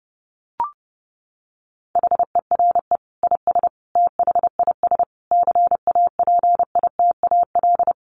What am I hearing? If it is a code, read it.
Morse code, heard as a keyed tone, about 30 wpm, 700 Hz. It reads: HERE IS THIS CAPITAL
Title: Sentences from Top 1000 Words 30wpm